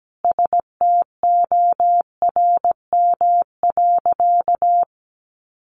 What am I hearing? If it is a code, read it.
Morse code, heard as a keyed tone, about 17 wpm, 700 Hz. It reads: STORM.
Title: Sentences from Top 1500 Words 17wpm